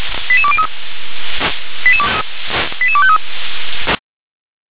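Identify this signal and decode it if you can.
Five-tone selective call (ZVEI), heard as quick runs of five tones, tones of about 70 ms. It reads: 88203 88253
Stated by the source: ZVEI five-tone selective call 88182 received on an SDR.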